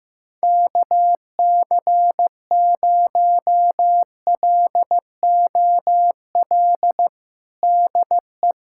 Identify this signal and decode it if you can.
Morse code, heard as a keyed tone, about 15 wpm, 700 Hz. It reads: KC0LOL DE
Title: USA Callsign DE Callsign 15wpm